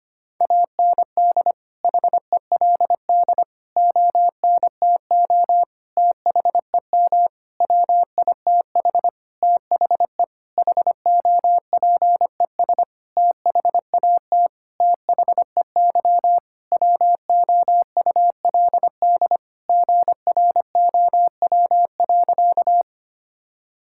Morse code, heard as a keyed tone, about 25 wpm, 700 Hz. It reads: AND HELD ONTO THEM WITH THE HOPES THAT THEY WOULD GROW.
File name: practice / 25wpm / Sentences with 9-Word Phrases - 25wpm